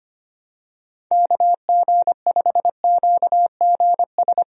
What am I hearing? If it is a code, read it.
Morse code, heard as a keyed tone, about 25 wpm, 700 Hz. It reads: KG5QGS
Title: USA Call Signs - 6 Characters - Repeated 3x - 25wpm